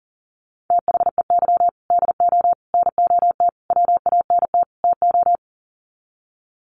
Morse code, heard as a keyed tone, about 40 wpm, 700 Hz. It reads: THEY DO NOT WANT TO
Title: Sets of 5 Words 40wpm